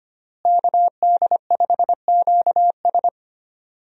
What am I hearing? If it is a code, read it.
Morse code, heard as a keyed tone, about 25 wpm, 700 Hz. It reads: KD5QS
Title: USA Call Signs - 5 Characters - Repeated 2x - 25wpm